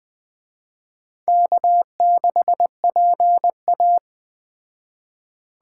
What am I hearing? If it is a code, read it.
Morse code, heard as a keyed tone, about 19 wpm, 700 Hz. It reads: K6PA